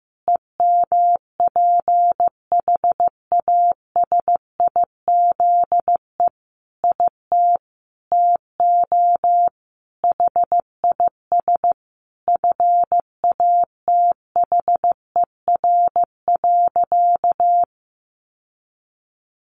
Morse code, heard as a keyed tone, about 15 wpm, 700 Hz. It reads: EMPHASIZE IT TO HIS FATHER.
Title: Sentences from Top 1500 Words 15wpm